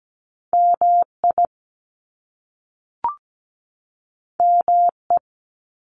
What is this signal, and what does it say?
Morse code, heard as a keyed tone, about 17 wpm, 700 Hz. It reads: MI ME
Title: US States 17wpm